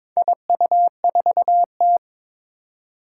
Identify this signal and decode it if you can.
Morse code, heard as a keyed tone, about 22 wpm, 700 Hz. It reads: IU4T
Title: Four Challenging Letter-Number Combinations 22wpm